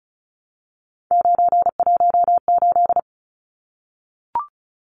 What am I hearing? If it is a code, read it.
Morse code, heard as a keyed tone, about 34 wpm, 700 Hz. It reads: 918
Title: Numbers - 3-Digit 35wpm